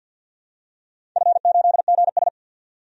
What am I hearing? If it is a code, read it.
Morse code, heard as a keyed tone, about 52 wpm, 700 Hz. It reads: F8GS
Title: DXCC-100 Call Signs - 4 Characters - 50wpm